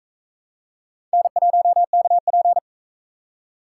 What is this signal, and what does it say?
Morse code, heard as a keyed tone, about 41 wpm, 700 Hz. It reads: N1KP